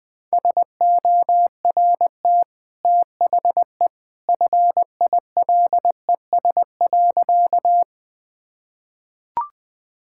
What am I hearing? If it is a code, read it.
Morse code, heard as a keyed tone, about 21 wpm, 700 Hz. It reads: SORT THE FILES.
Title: Easy Sentences with Hints - 20wpm